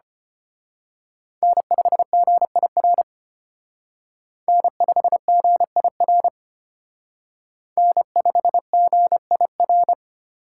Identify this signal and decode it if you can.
Morse code, heard as a keyed tone, about 29 wpm, 700 Hz. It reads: N5GIR N5GIR N5GIR
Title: Speed Racing - USA 1x3 Call Signs 25wpm